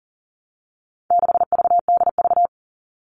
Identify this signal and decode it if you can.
Morse code, heard as a keyed tone, about 40 wpm, 700 Hz. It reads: 6VDV
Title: All Cut Numbers - 4-Digit 40wpm